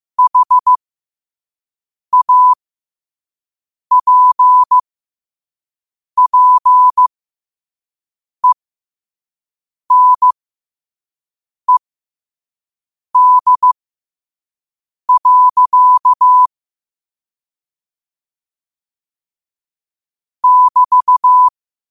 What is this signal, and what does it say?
Morse code, heard as a keyed tone, about 15 wpm, spaced as for 5 wpm, 1000 Hz. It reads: HAPPENED. <BT>
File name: qotdfiles / 5wpm